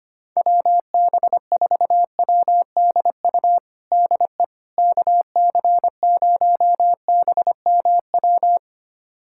Morse code, heard as a keyed tone, about 25 wpm, 700 Hz. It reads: WB4WDU DE KC0BMW
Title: USA Callsign DE Callsign - 6 Characters - 25wpm